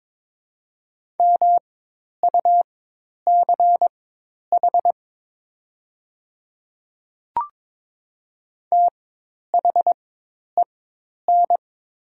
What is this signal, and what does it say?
Morse code, heard as a keyed tone, about 21 wpm, spaced as for 10 wpm, 700 Hz. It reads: MUCH THEN
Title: 4x Character Spacing - Top 200 Words - 22wpm